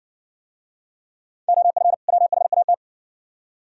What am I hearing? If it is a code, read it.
Morse code, heard as a keyed tone, about 60 wpm, 690 Hz. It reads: OF WHAT